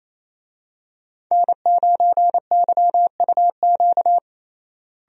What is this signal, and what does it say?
Morse code, heard as a keyed tone, about 28 wpm, 700 Hz. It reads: N9YUQ
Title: USA Call Signs - 5 Characters - Repeated 2x - 28wpm